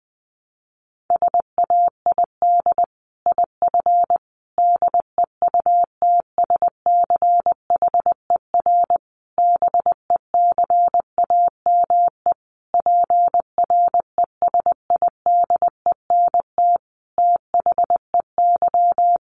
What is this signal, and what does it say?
Morse code, heard as a keyed tone, about 20 wpm, 700 Hz. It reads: SAID IF DEUTSCHER BECAME PRESIDENT THEY
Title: A Sound Of Thunder 20wpm